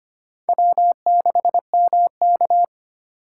Morse code, heard as a keyed tone, about 25 wpm, 700 Hz. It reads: W6MK
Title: USA Call Signs - 4 Characters - 2x Character Spacing - 25wpm